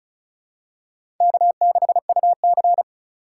Morse code, heard as a keyed tone, about 36 wpm, 700 Hz. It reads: K6UC
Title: USA Call Signs - 4 Characters - 2x Character Spacing - 35wpm